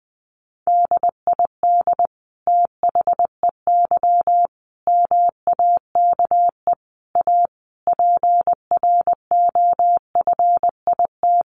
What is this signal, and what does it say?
Morse code, heard as a keyed tone, about 20 wpm, 700 Hz. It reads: DID THEY MAKE A PROFIT
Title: Sentences from Top 800 Words 20wpm